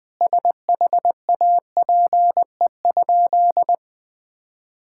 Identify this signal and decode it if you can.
Morse code, heard as a keyed tone, about 21 wpm, 700 Hz. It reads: SHAPE?
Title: Sentences from Top 1000 Words - Review Part 6 - 20wpm